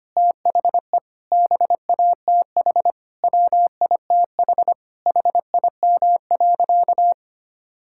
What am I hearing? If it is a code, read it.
Morse code, heard as a keyed tone, about 25 wpm, 700 Hz. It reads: THE BATH WITH HIM.
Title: Sentences from Top 1600 Words 25wpm